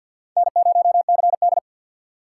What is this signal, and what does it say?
Morse code, heard as a keyed tone, about 48 wpm, 700 Hz. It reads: N0CD
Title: USA Call Signs - 4 Characters - 6x Character Spacing - 50wpm